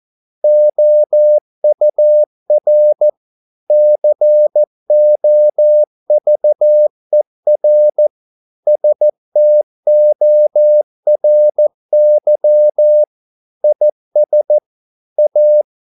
Morse code, heard as a keyed tone, about 14 wpm, 600 Hz. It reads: OUR COVER STORY IS A